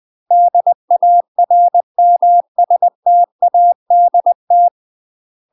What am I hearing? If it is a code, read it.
Morse code, heard as a keyed tone, about 20 wpm, 700 Hz. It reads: DARMSTADT
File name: UK MASTER QSOs TRAINING / 20WPM / UK MASTER QSOs 0055